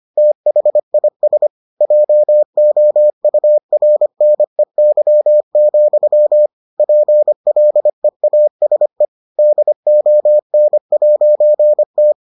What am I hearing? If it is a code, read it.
Morse code, heard as a keyed tone, about 25 wpm, 600 Hz. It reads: THIS JOURNEY, PLEASE DON'T